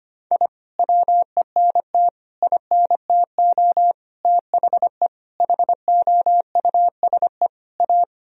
Morse code, heard as a keyed tone, about 25 wpm, 700 Hz. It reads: I WENT INTO THE HOUSE A
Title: Sentences from Top 1000 Words - Review Part 1 - 25wpm